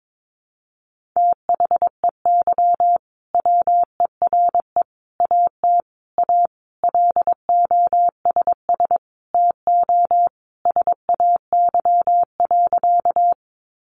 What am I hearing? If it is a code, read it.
Morse code, heard as a keyed tone, about 22 wpm, 700 Hz. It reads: THEY WERE AT A LOSS TO SAY.
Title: Sentences from Top 1400 Words 22wpm